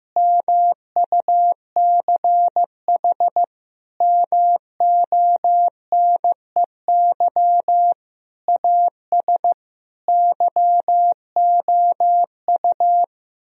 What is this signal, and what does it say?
Morse code, heard as a keyed tone, about 15 wpm, 700 Hz. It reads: MUCH MONEY AS YOU
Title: Sentences from Top 1000 Words - Review Part 3 - 15wpm